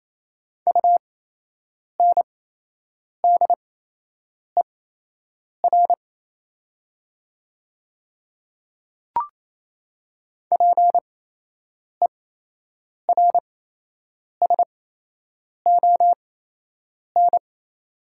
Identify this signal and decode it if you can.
Morse code, heard as a keyed tone, about 28 wpm, spaced as for 8 wpm, 700 Hz. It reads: UNDER PERSON
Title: 8x Character Spacing - Top 500 Words - 28wpm